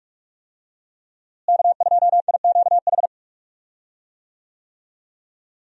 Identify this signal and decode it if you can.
Morse code, heard as a keyed tone, about 46 wpm, 700 Hz. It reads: K2IQH
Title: USA Call Signs - 5 Characters - Repeated 3x - 45wpm